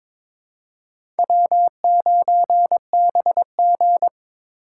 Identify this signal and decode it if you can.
Morse code, heard as a keyed tone, about 22 wpm, 700 Hz. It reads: W9BG
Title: USA Call Signs - 4 Characters - 3x Character Spacing - 22wpm